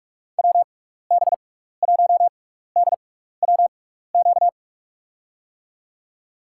Morse code, heard as a keyed tone, about 46 wpm, spaced as for 16 wpm, 700 Hz. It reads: WB1DWQ
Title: USA Call Signs - 6 Characters - 6x Character Spacing - 45wpm